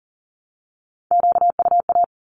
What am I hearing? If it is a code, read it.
Morse code, heard as a keyed tone, about 40 wpm, 700 Hz. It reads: QUA
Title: Q-Codes 40wpm